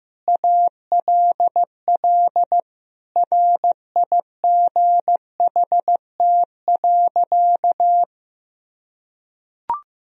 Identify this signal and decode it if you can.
Morse code, heard as a keyed tone, about 15 wpm, 700 Hz. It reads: ALL RIGHT.
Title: Sentences from Top 1000 Words - Review Part 1 - 15wpm